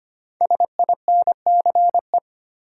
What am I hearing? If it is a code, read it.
Morse code, heard as a keyed tone, about 25 wpm, 700 Hz. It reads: SINCE
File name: 5 Letter Words - First 3 Letters Given - 25wpm